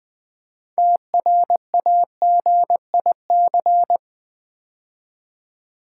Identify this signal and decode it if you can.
Morse code, heard as a keyed tone, about 20 wpm, 700 Hz. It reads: TRAGIC
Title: 6 Letter Words - First 3 Letters Given - 20wpm